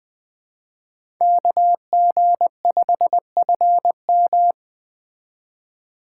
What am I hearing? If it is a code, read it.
Morse code, heard as a keyed tone, about 20 wpm, 700 Hz. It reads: KG5FM
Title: Speed Racing - USA 2x2 Call Signs 17wpm